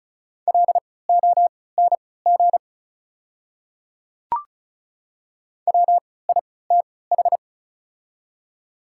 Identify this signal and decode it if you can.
Morse code, heard as a keyed tone, about 36 wpm, spaced as for 20 wpm, 690 Hz. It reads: LONG WITH
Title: 3x Character Spacing - Top 100 Words - 35wpm